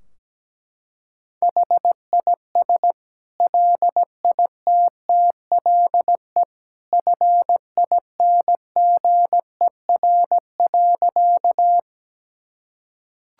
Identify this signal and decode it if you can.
Morse code, heard as a keyed tone, about 17 wpm, 700 Hz. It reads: HIS LITTLE FINGER.
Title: Sentences from Top 1600 Words 17wpm